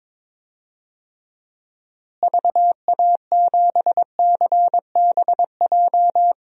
Morse code, heard as a keyed tone, about 22 wpm, 700 Hz. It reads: VA7CBJ
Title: DXCC-100 Call Signs - 6 Characters - Repeated 3x - 22wpm